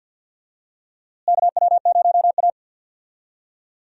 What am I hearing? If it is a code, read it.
Morse code, heard as a keyed tone, about 46 wpm, 700 Hz. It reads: KW0A
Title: USA Call Signs - 4 Characters - Repeated 3x - 50wpm